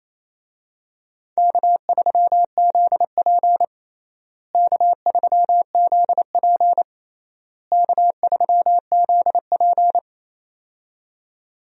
Morse code, heard as a keyed tone, about 28 wpm, 700 Hz. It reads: K3ZP K3ZP K3ZP
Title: USA Call Signs - 4 Characters - Repeated 3x - 28wpm